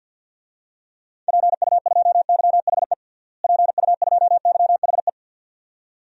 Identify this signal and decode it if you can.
Morse code, heard as a keyed tone, about 49 wpm, 700 Hz. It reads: PU2YHE PU2YHE
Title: DXCC-100 Call Signs - 6 Characters - Repeated 2x - 50wpm